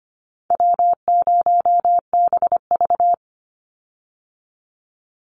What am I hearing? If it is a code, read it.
Morse code, heard as a keyed tone, about 25 wpm, 700 Hz. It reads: W0BV
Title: Easy Call Signs - 25wpm